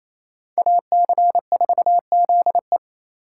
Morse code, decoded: AC4ZE

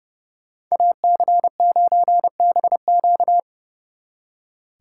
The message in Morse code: AC9BQ

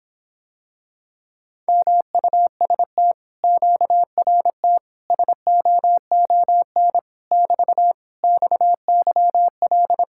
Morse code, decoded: MUST QRT SOON <BT> XYL

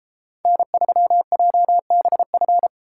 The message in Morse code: N3JBF

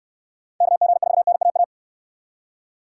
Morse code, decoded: KG4NAA

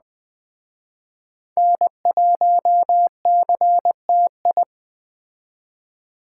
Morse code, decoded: N1CTI

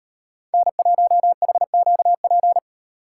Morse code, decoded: N1HQP

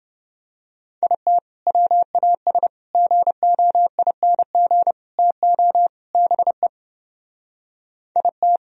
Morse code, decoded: IT WAS GOING TO BE IT

